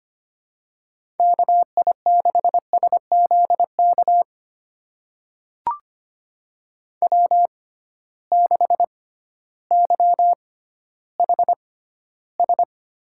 Morse code, heard as a keyed tone, about 25 wpm, spaced as for 13 wpm, 700 Hz. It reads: KI6SZK W6YHS
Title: USA Call Signs - 6x Character Spacing - 25wpm